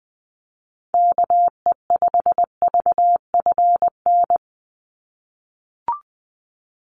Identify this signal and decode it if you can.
Morse code, heard as a keyed tone, about 20 wpm, 700 Hz. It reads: KE5VFN
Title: USA Call Signs 20wpm